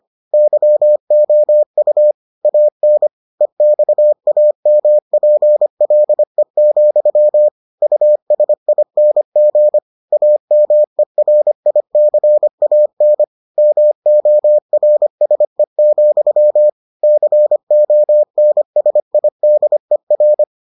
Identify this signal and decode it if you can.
Morse code, heard as a keyed tone, about 25 wpm, 600 Hz. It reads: YOU AN EXAMPLE, USING AMERICAN MORSE, CONSIDER